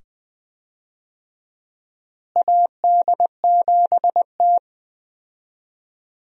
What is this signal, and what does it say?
Morse code, heard as a keyed tone, about 21 wpm, 700 Hz. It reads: AD7T
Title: USA Call Signs - 4 Characters - Repeated 2x - 20wpm